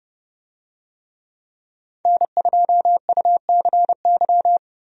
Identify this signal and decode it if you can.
Morse code, heard as a keyed tone, about 32 wpm, 700 Hz. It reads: N2UCY
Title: USA Call Signs - 5 Characters - 30wpm